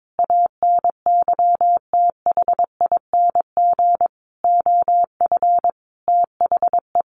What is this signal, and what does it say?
Morse code, heard as a keyed tone, about 22 wpm, 700 Hz. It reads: ANYTHING OF THE
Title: Sentences from Top 1000 Words 22wpm